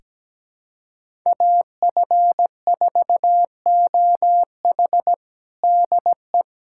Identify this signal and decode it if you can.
Morse code, heard as a keyed tone, about 17 wpm, 700 Hz. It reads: AF4OH DE
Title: USA Callsign DE Callsign 17wpm